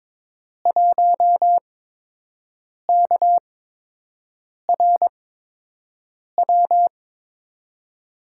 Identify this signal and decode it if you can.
Morse code, heard as a keyed tone, about 22 wpm, spaced as for 6 wpm, 700 Hz. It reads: 1KRW
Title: USA Call Signs - 5 Characters - 8x Character Spacing - 22wpm